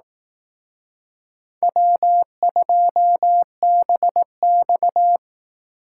Morse code, heard as a keyed tone, about 18 wpm, 700 Hz. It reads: W2BX
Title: Speed Racing - USA 1x2 Call Signs 15wpm